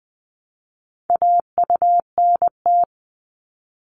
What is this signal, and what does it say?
Morse code, heard as a keyed tone, about 20 wpm, 700 Hz. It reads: AUNT
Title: Multi-letter Phonemes Repeated 3x 20wpm